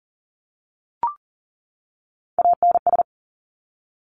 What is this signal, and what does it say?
Morse code, heard as a keyed tone, about 38 wpm, 700 Hz. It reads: ANS
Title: Top Three Letter Combinations in Words 40wpm